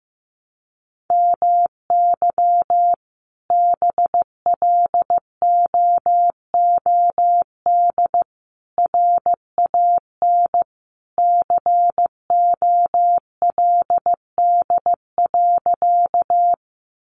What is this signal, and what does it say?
Morse code, heard as a keyed tone, about 15 wpm, 700 Hz. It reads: MY BLOOD RAN COLD.
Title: Sentences from Top 1100 Words 15wpm